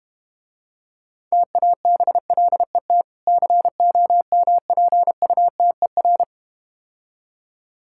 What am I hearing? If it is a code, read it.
Morse code, heard as a keyed tone, about 32 wpm, 700 Hz. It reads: TABLET COMPUTER